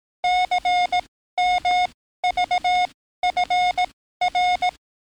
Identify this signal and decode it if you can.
Morse code, heard as a keyed tone, about 18 wpm, spaced as for 13 wpm, 720 Hz. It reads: CMVFR
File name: training / NAVPERS 11049to11060 / NAVPERS 11056 RA RB